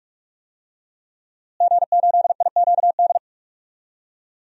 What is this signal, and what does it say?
Morse code, heard as a keyed tone, about 43 wpm, 700 Hz. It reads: G8IQD
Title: DXCC-100 Call Signs - 5 Characters - Repeated 2x - 45wpm